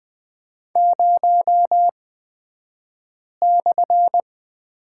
Morse code, heard as a keyed tone, about 20 wpm, spaced as for 5 wpm, 700 Hz. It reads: 0/
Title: 254 Lesson 41d - Period - Two Characters 20wpm